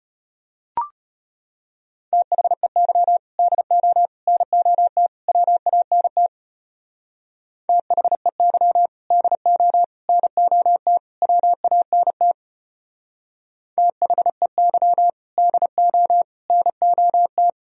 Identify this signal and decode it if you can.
Morse code, heard as a keyed tone, about 31 wpm, 700 Hz. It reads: THEY DO NOT WANT THEY DO NOT WANT THEY DO NOT
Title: Speed-Racing - Sets of 4 Words 25wpm